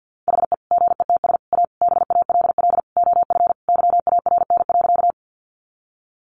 Morse code, heard as a keyed tone, about 50 wpm, 700 Hz. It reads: HE GETS A BALL OF YARN.
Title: Additional Easy Phrases 50wpm